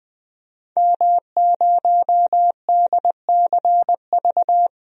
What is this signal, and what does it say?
Morse code, heard as a keyed tone, about 20 wpm, 700 Hz. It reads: M0DCV
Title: DXCC-100 Call Signs - 5 Characters - 20wpm